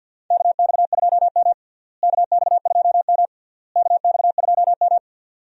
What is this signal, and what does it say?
Morse code, heard as a keyed tone, about 52 wpm, 700 Hz. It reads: KX2M KX2M KX2M
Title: USA Call Signs - 4 Characters - Repeated 3x - 50wpm